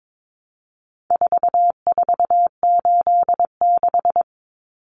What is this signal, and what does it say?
Morse code, heard as a keyed tone, about 22 wpm, 700 Hz. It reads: 4486
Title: Numbers - 4-Digit 22wpm